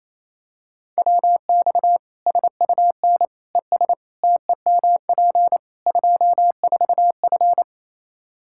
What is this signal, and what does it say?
Morse code, heard as a keyed tone, about 28 wpm, 700 Hz. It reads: WX SUN ES TEMP 24F
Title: Practice QSO - Exchange 2 - Info - 28wpm